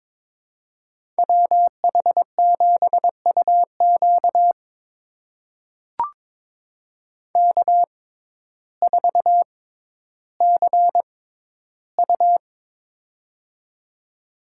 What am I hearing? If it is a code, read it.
Morse code, heard as a keyed tone, about 22 wpm, spaced as for 12 wpm, 700 Hz. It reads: WH7UQ K4CU
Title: USA Call Signs - 6x Character Spacing - 22wpm